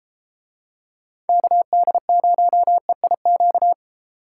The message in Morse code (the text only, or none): KD0EIQ